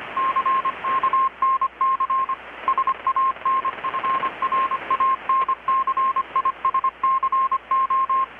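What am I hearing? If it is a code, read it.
Morse code, heard as a keyed tone, about 25 wpm, 1060 Hz, with radio noise. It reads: CK NC SAN FRANCISCO
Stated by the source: Morse from KPH's Night of Nights XX maritime radiotelegraphy event, received in Chiapas, Mexico about 3,800 km away on a Kenwood R-600 with a grounded 10 m long-wire antenna.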